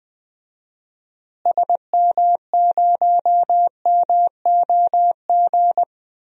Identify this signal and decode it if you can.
Morse code, heard as a keyed tone, about 21 wpm, 700 Hz. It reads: SM0MOG
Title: DXCC-100 Call Signs - 6 Characters - Repeated 2x - 20wpm